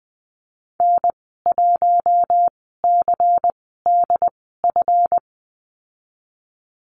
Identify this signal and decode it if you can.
Morse code, heard as a keyed tone, about 20 wpm, spaced as for 14 wpm, 700 Hz. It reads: N1CDF
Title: 147 Lesson 24g - Letter y - Call Signs 20wpm